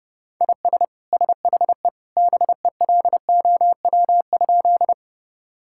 Morse code, heard as a keyed tone, about 30 wpm, 700 Hz. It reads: IS SHE BELOW?
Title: Sentences from Top 1000 Words - Review Part 5 - 30wpm